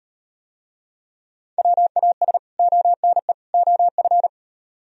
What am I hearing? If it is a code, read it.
Morse code, heard as a keyed tone, about 38 wpm, 700 Hz. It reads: WAS ONE OF